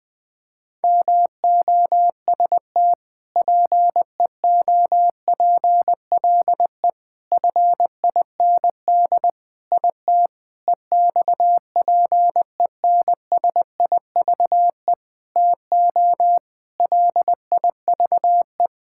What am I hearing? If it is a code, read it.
Morse code, heard as a keyed tone, about 19 wpm, 700 Hz. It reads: MOST PEOPLE FIND IT EXPENSIVE TO LIVE
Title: Sentences from Top 1000 Words - Review Part 6 - 20wpm